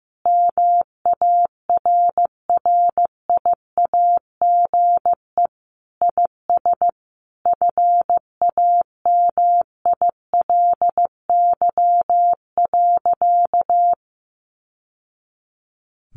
Morse code, heard as a keyed tone, about 15 wpm, 700 Hz. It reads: MARRIAGE IS FAMILY.
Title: Sentences from Top 800 Words 15wpm